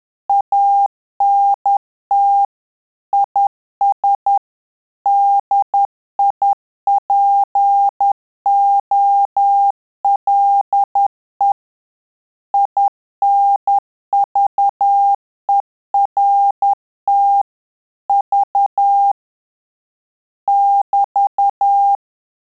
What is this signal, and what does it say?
Morse code, heard as a keyed tone, about 10 wpm, 800 Hz. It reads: ANT IS DIPOLE INVERT V <BT>
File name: Telegraphy-group / QSO CW 10 PPM